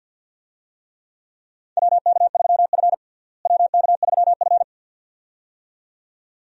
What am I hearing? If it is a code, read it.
Morse code, heard as a keyed tone, about 52 wpm, 700 Hz. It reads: WK3F WK3F
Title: USA Call Signs - 4 Characters - Repeated 2x - 50wpm